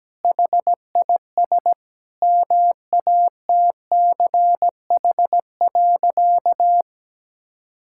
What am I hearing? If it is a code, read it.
Morse code, heard as a keyed tone, about 17 wpm, 700 Hz. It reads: HIS MATCH.
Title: Sentences from Top 1600 Words 17wpm